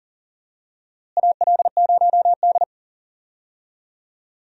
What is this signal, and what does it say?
Morse code, heard as a keyed tone, about 40 wpm, 700 Hz. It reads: AL0D
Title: USA Call Signs - 4 Characters - Repeated 2x - 40wpm